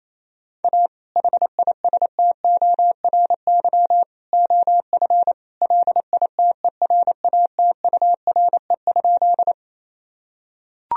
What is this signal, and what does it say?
Morse code, heard as a keyed tone, about 28 wpm, 700 Hz. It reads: A HISTORY OF LITERATURE?
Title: Sentences from Top 1000 Words - Review Part 3 - 28wpm